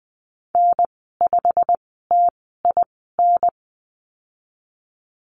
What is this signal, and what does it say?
Morse code, heard as a keyed tone, about 20 wpm, spaced as for 15 wpm, 700 Hz. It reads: N5TIN
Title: 128 Lesson 21g - Number 6 - Call Signs 20wpm